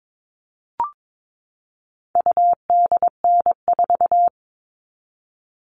Morse code, heard as a keyed tone, about 22 wpm, 700 Hz. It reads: UDN4
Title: All Cut Numbers - 4-Digit 22wpm